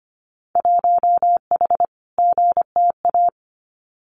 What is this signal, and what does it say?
Morse code, heard as a keyed tone, about 25 wpm, 700 Hz. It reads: 1H GTA
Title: Winter Field Day Exchanges 25wpm